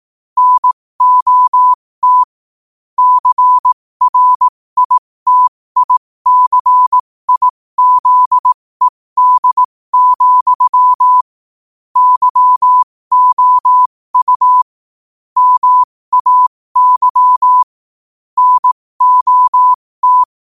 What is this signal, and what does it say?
Morse code, heard as a keyed tone, about 18 wpm, spaced as for 15 wpm, 1000 Hz. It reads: NOT CRITICIZED, YOU MAY NOT
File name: qotdfiles / 15wpm